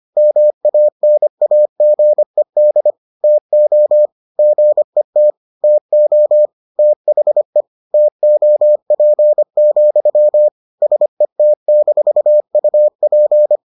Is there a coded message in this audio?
Yes: Morse code from 0.2 to 13.6 s, heard as a keyed tone, about 25 wpm, 600 Hz. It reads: MANAGED TO GET TO THE TOP, SET-UP